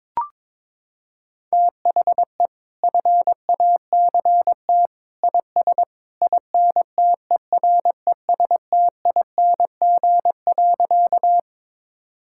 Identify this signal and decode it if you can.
Morse code, heard as a keyed tone, about 22 wpm, 700 Hz. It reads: THE FACT IS INTERESTING.
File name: practice / 22wpm / Sentences with 2-Word Phrases - 22wpm